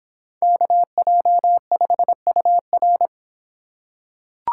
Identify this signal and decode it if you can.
Morse code, heard as a keyed tone, about 25 wpm, 700 Hz. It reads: KJ5UR